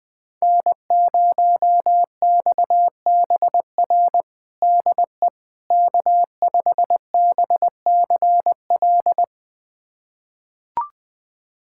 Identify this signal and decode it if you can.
Morse code, heard as a keyed tone, about 20 wpm, 700 Hz. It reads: N0XBR DE K5BCL
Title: USA Callsign DE Callsign - 5 Characters - 20wpm